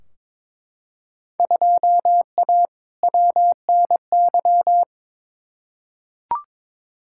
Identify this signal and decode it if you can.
Morse code, heard as a keyed tone, about 22 wpm, 700 Hz. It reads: 2A WNY
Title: Field Day Exchanges - 22wpm